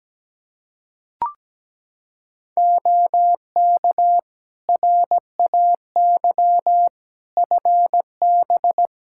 Morse code, heard as a keyed tone, about 17 wpm, 700 Hz. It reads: OK RAY FB
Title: Practice QSO - Exchange 2 - Info - 17wpm